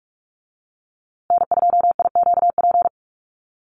Morse code, heard as a keyed tone, about 44 wpm, 700 Hz. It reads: N2IQP
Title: USA Call Signs Repeated 3x 45wpm